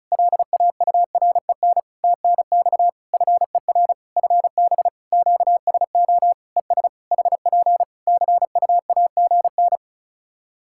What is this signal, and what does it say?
Morse code, heard as a keyed tone, about 34 wpm, 700 Hz. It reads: LAUREN TNX FER FB QSO ES HP CUAGN
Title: Practice QSO - Exchange 4 - Ending - 35wpm